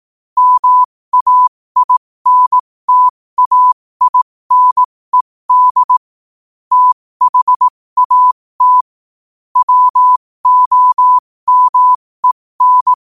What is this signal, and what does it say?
Morse code, heard as a keyed tone, about 18 wpm, spaced as for 15 wpm, 1000 Hz. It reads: MAINTAINED THAT WOMEN
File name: qotdfiles / 15wpm